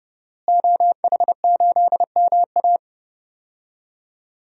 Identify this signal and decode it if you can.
Morse code, heard as a keyed tone, about 30 wpm, 700 Hz. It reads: OH8MA
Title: DXCC-100 Call Signs - 5 Characters - Repeated 3x - 30wpm